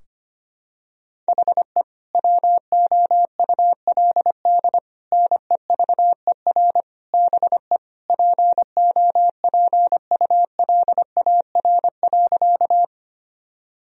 Morse code, heard as a keyed tone, about 25 wpm, 700 Hz. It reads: HE WOULD NEVER BE POPULAR.
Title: Sentences from Top 1000 Words - Review Part 3 - 25wpm